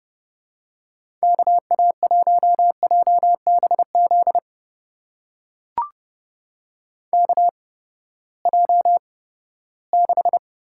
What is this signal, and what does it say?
Morse code, heard as a keyed tone, about 30 wpm, spaced as for 17 wpm, 700 Hz. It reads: KA1JBZ KJ6
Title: USA Call Signs - 6 Characters - 8x Character Spacing - 30wpm